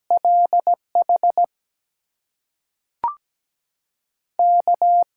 Morse code, heard as a keyed tone, about 17 wpm, spaced as for 5 wpm, 700 Hz. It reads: LHK
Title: USA Call Signs - 6 Characters - 6x Character Spacing - 17wpm